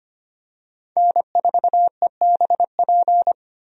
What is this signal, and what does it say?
Morse code, heard as a keyed tone, about 25 wpm, 700 Hz. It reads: N4EBP
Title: USA Call Signs - 5 Characters - 25wpm